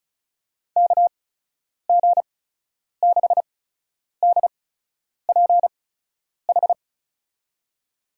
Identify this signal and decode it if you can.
Morse code, heard as a keyed tone, about 36 wpm, spaced as for 10 wpm, 700 Hz. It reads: KG6DPH
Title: USA Call Signs - 6 Characters - 8x Character Spacing - 35wpm